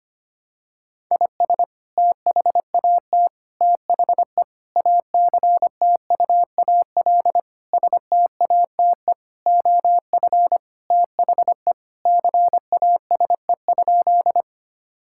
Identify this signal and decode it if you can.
Morse code, heard as a keyed tone, about 25 wpm, 700 Hz. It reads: IS THAT THE ACTUAL STATE OF THE CASE?
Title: Sentences from Top 1000 Words - Review Part 5 - 25wpm